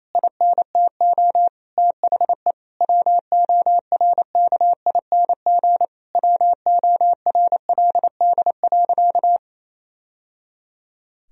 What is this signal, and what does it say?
Morse code, heard as a keyed tone, about 28 wpm, 700 Hz. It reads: INTO THE WORKING WORLD.
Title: Sentences from Top 1000 Words - Review Part 4 - 28wpm